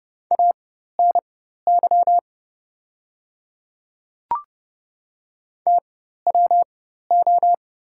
Morse code, heard as a keyed tone, about 30 wpm, spaced as for 14 wpm, 690 Hz. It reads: ANY TWO
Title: 4x Character Spacing - Top 500 Words - 30wpm